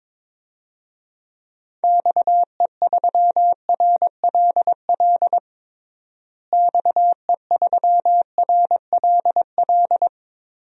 Morse code, heard as a keyed tone, about 21 wpm, 700 Hz. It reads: XE3RLL XE3RLL
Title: DXCC-100 Call Signs - 6 Characters - Repeated 3x - 22wpm